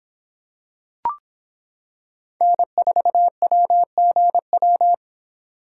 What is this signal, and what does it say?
Morse code, heard as a keyed tone, about 26 wpm, 700 Hz. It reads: N4WGW